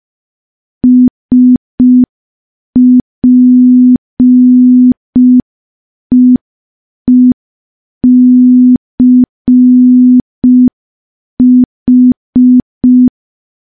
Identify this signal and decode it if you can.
Morse code, heard as a keyed tone, about 5 wpm, 250 Hz. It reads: SPEECH
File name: morse / morse-training-05WPM